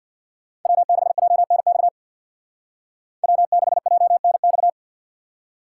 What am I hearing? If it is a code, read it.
Morse code, heard as a keyed tone, about 59 wpm, 700 Hz. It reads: W6JNX W6JNX
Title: Speed Racing - USA Call Signs 50wpm